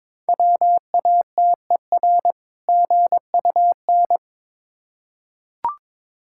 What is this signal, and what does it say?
Morse code, heard as a keyed tone, about 22 wpm, 690 Hz. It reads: WATER GUN
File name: Difficult Words with Hints - 22wpm